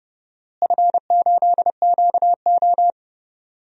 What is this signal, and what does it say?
Morse code, heard as a keyed tone, about 31 wpm, 700 Hz. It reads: F8QO
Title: DXCC-100 Call Signs - 4 Characters - Repeated 3x - 30wpm